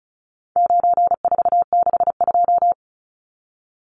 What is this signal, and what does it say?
Morse code, heard as a keyed tone, about 34 wpm, 700 Hz. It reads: 9462
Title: Numbers - 4-Digit 35wpm